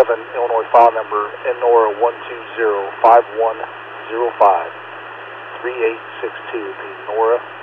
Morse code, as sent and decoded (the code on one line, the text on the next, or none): none